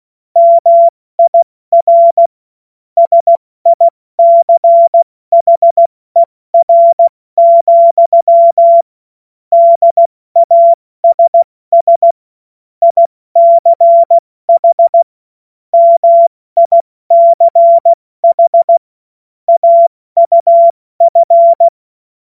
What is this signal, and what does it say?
Morse code, heard as a keyed tone, about 16 wpm, 680 Hz. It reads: MIR SICHER, DASS ICH MICH AUF
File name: Morsekurs / Witze / Witz 04. April 70 F80